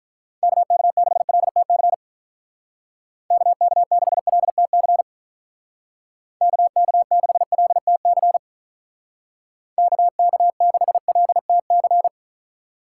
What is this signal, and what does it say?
Morse code, heard as a keyed tone, about 43 wpm, 700 Hz. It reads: KK6LTC KK6LTC KK6LTC KK6LTC
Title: Speed Racing - USA 2x3 Call Signs 35wpm